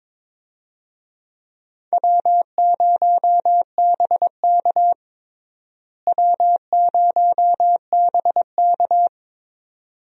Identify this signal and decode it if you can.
Morse code, heard as a keyed tone, about 22 wpm, 700 Hz. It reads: W0BK W0BK
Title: USA Call Signs - 4 Characters - Repeated 2x - 22wpm